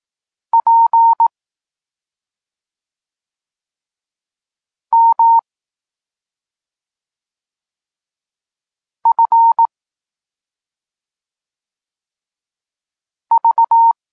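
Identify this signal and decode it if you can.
Morse code, heard as a keyed tone, about 18 wpm, 920 Hz. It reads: P M F V
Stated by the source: Polish Morse lesson teaching the letter C, from SP9MOA Niepołomicki Klub Krótkofalowców.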